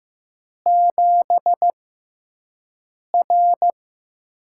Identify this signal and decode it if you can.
Morse code, heard as a keyed tone, about 15 wpm, spaced as for 5 wpm, 700 Hz. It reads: 7R
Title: USA Call Signs - 6x Character Spacing - 15wpm